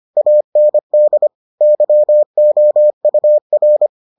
Morse code, heard as a keyed tone, about 25 wpm, 600 Hz. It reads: AND YOUR